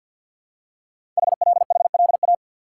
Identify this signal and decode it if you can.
Morse code, heard as a keyed tone, about 50 wpm, 700 Hz. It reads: HLHLA
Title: Code Groups - 5 Letters - 50wpm